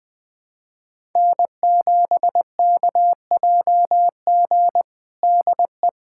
Morse code, heard as a keyed tone, about 20 wpm, 700 Hz. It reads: N7KJG DE
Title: USA Callsign DE Callsign - 5 Characters - 20wpm